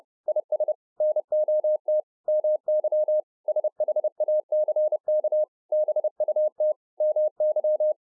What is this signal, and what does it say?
Morse code, heard as a keyed tone, about 30 wpm, 610 Hz. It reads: IS NOT MY SHACK BUT MY